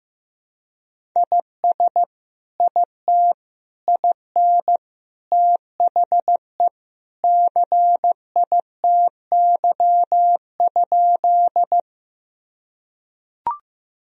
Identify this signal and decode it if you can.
Morse code, heard as a keyed tone, about 15 wpm, 700 Hz. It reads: IS IT IN THE CITY?
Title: Sentences from Top 1000 Words - Review Part 1 - 15wpm